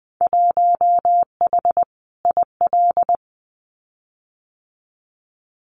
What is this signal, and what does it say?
Morse code, heard as a keyed tone, about 20 wpm, 700 Hz. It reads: 1H IL
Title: Winter Field Day Exchanges 20wpm